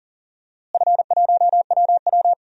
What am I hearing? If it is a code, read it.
Morse code, heard as a keyed tone, about 40 wpm, 700 Hz. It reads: F1WW
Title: DXCC-100 Call Signs - 4 Characters - Repeated 3x - 40wpm